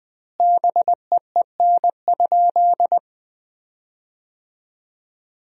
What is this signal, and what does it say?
Morse code, heard as a keyed tone, about 20 wpm, 700 Hz. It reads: BEEN?
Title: 4x Word-Spacing - Sentences from Top 200 Words - 20wpm